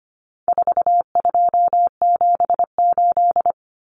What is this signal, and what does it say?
Morse code, heard as a keyed tone, about 25 wpm, 700 Hz. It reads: 4278
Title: Numbers - 4-Digit 25wpm